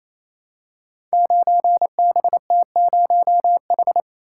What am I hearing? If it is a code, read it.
Morse code, heard as a keyed tone, about 28 wpm, 700 Hz. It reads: 9BT0H
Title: Code Groups - 5 Letter:Numbers - 28wpm